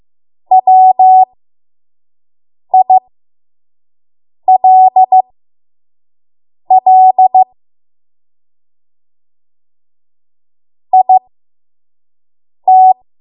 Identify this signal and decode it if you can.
Morse code, heard as a keyed tone, about 15 wpm, spaced as for 5 wpm, 750 Hz. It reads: WILL IT
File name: Morse / 05WPM